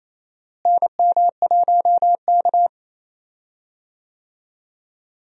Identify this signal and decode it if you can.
Morse code, heard as a keyed tone, about 28 wpm, 700 Hz. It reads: NM1K
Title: USA Call Signs - 4 Characters - Repeated 3x - 28wpm